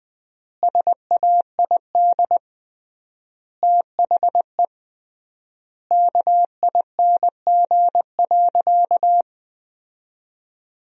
Morse code, heard as a keyed tone, about 20 wpm, 700 Hz. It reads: SAID THE KING.
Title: 3x Word-Spacing - Sentences from Top 400 Words - 20wpm